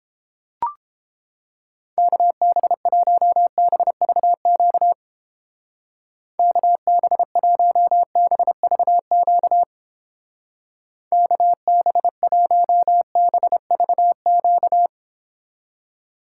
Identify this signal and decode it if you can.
Morse code, heard as a keyed tone, about 31 wpm, 700 Hz. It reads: KB1BVQ KB1BVQ KB1BVQ